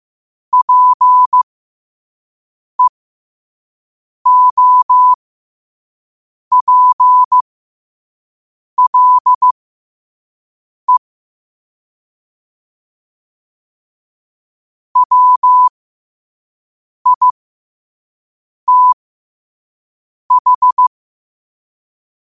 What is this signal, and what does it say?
Morse code, heard as a keyed tone, about 15 wpm, spaced as for 5 wpm, 1000 Hz. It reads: PEOPLE WITH
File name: qotdfiles / 5wpm